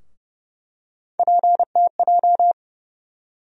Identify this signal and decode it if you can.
Morse code, heard as a keyed tone, about 30 wpm, 700 Hz. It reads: PTJ
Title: USA Call Sign Postfixes - 3 Letters 30wpm